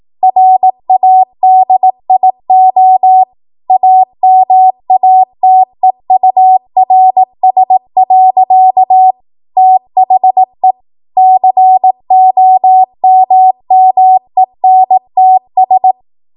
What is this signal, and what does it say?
Morse code, heard as a keyed tone, about 18 wpm, 750 Hz. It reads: RADIO AMATEURS. THE COMMENTS
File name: Morse / 18WPM